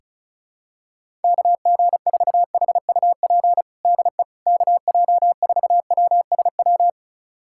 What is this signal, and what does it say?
Morse code, heard as a keyed tone, about 36 wpm, 700 Hz. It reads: KG4HUP DE KJ4WSW
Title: USA Callsign DE Callsign - 6 Characters - 35wpm